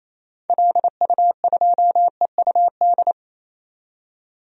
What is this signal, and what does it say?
Morse code, heard as a keyed tone, about 28 wpm, 700 Hz. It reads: LU2EUD
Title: DXCC-100 Call Signs - 6 Characters - 28wpm